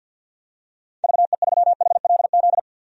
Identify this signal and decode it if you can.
Morse code, heard as a keyed tone, about 52 wpm, 700 Hz. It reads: VE3HLZ
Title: DXCC-100 Call Signs - 6 Characters - 50wpm